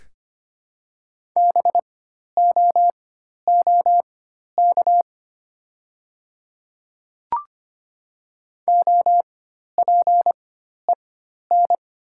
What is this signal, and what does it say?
Morse code, heard as a keyed tone, about 25 wpm, spaced as for 12 wpm, 700 Hz. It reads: BOOK OPEN
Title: 4x Character Spacing - Top 300 Words - 25wpm